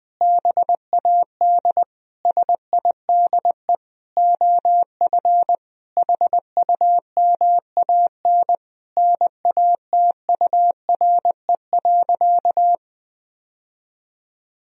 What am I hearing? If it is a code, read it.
Morse code, heard as a keyed tone, about 21 wpm, 700 Hz. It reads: BAD SIDE OF HUMAN NATURE.
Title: Sentences from Top 1000 Words - Review Part 3 - 20wpm